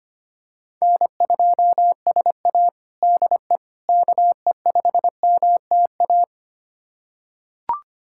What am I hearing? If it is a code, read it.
Morse code, heard as a keyed tone, about 25 wpm, 700 Hz. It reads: N2SA DE KE5MTA
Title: USA Callsign DE Callsign 25wpm